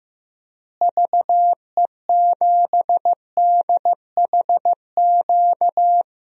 Morse code, heard as a keyed tone, about 15 wpm, 700 Hz. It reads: VE7DHQ
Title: DXCC-100 Call Signs - 6 Characters - Repeated 3x - 15wpm